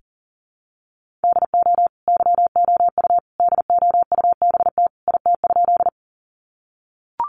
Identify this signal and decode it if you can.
Morse code, heard as a keyed tone, about 40 wpm, 700 Hz. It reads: DO YOU DOUBT IT?
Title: Sentences from Top 1500 Words 40wpm